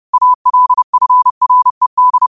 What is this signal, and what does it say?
Morse code, heard as a keyed tone, about 30 wpm, 1000 Hz. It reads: ALFRED
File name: qotdfiles / 30wpm